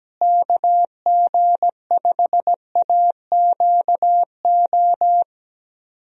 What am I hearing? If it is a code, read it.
Morse code, heard as a keyed tone, about 17 wpm, 700 Hz. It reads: KG5AQO
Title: USA Call Signs - 6 Characters - Repeated 3x - 17wpm